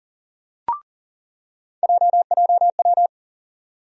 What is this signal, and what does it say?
Morse code, heard as a keyed tone, about 40 wpm, 700 Hz. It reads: JJW
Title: USA Call Sign Postfixes - 3 Letters 40wpm